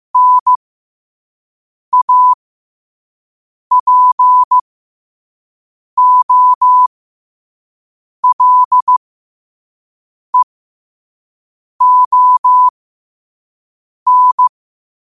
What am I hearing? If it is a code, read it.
Morse code, heard as a keyed tone, about 15 wpm, spaced as for 5 wpm, 1000 Hz. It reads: NAPOLEON
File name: qotdfiles / 5wpm